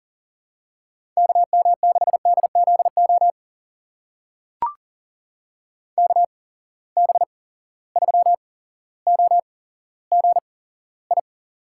Morse code, heard as a keyed tone, about 40 wpm, spaced as for 17 wpm, 700 Hz. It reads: KM6DZO KB3OGI
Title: USA Call Signs - 6 Characters - 8x Character Spacing - 40wpm